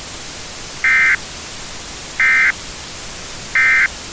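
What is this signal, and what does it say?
Emergency Alert System (SAME) tones, heard as harsh buzzing bursts. It is heard from 0.7 to 3.9 s.
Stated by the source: Emergency alert tones used in a game map.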